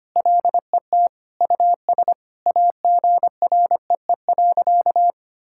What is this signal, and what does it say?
Morse code, heard as a keyed tone, about 25 wpm, 700 Hz. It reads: LET US AGREE.
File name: practice / 25wpm / Sentences with 4-Word Phrases - 25wpm